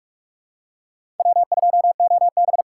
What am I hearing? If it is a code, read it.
Morse code, heard as a keyed tone, about 47 wpm, 700 Hz. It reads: W2OB
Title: USA Call Signs - 4 Characters - 3x Character Spacing - 45wpm